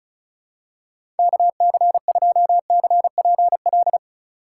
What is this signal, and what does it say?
Morse code, heard as a keyed tone, about 34 wpm, 700 Hz. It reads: KC2CPL
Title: USA Call Signs - 6 Characters - 3x Character Spacing - 35wpm